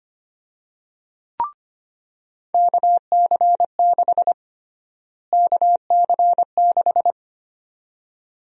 Morse code, heard as a keyed tone, about 25 wpm, 700 Hz. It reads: KC6 KC6
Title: USA Call Sign Prefixes - Repeated 2x - 25wpm